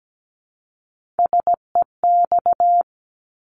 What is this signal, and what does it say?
Morse code, heard as a keyed tone, about 17 wpm, 700 Hz. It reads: SEX
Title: Multi-letter Phonemes Repeated 3x 17wpm